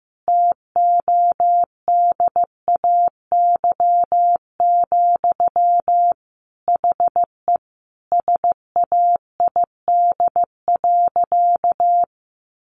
Morse code, heard as a keyed tone, about 15 wpm, 700 Hz. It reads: TODAY, HE SAID.
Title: Sentences from Top 600 Words 15wpm